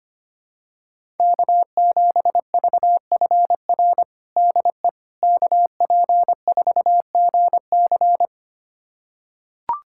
Morse code, heard as a keyed tone, about 25 wpm, 700 Hz. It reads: K7VFR DE KP4GC
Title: USA Callsign DE Callsign - 5 Characters - 25wpm